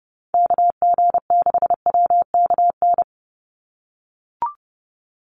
Morse code, heard as a keyed tone, about 30 wpm, 700 Hz. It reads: KG6WKN
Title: USA Call Signs 30wpm